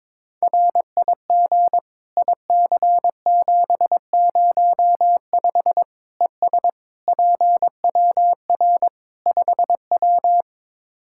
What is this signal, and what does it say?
Morse code, heard as a keyed tone, about 22 wpm, 700 Hz. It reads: RIG IC705 ES PWR 5W
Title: Practice QSO - Exchange 2 - Info - 22wpm